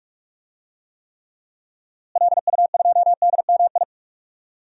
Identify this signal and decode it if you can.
Morse code, heard as a keyed tone, about 46 wpm, 700 Hz. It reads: LU2DMI
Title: DXCC-100 Call Signs - 6 Characters - Repeated 3x - 45wpm